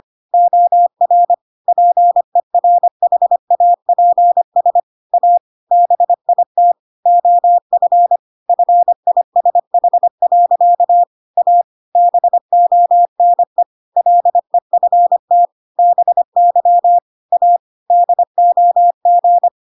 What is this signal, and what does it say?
Morse code, heard as a keyed tone, about 25 wpm, 700 Hz. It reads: OR PERHAPS A BIT OF FISH. A BONE LEFT BY A DOG
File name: Pinocchio / Pinocchio ch5 25 25 ik1hge